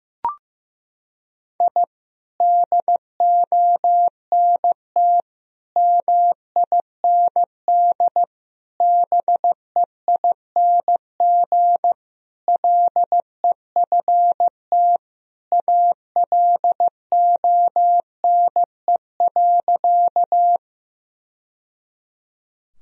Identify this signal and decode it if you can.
Morse code, heard as a keyed tone, about 15 wpm, 700 Hz. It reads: I DONT MIND BEING LEFT ALONE.
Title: Sentences from Top 1000 Words - Review Part 3 - 15wpm